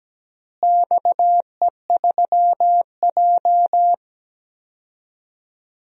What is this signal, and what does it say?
Morse code, heard as a keyed tone, about 17 wpm, 700 Hz. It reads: XE3J
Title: DXCC-100 Call Signs - 4 Characters - Repeated 2x - 17wpm